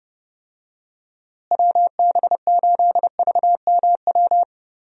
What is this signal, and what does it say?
Morse code, heard as a keyed tone, about 30 wpm, 700 Hz. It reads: WB8VMW
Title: USA Call Signs - 6 Characters - Repeated 2x - 30wpm